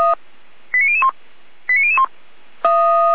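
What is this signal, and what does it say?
Five-tone selective call (ZVEI), heard as quick runs of five tones, tones of about 70 ms. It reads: 89001 89001
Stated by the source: ZVEI five-tone selective call received by SDR.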